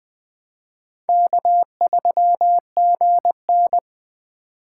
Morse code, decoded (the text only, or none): K3GN